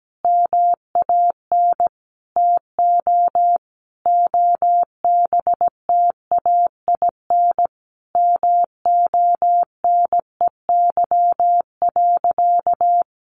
Morse code, heard as a keyed tone, about 17 wpm, 700 Hz. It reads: MAN TO OBTAIN MONEY.
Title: Sentences from Top 1300 Words 17wpm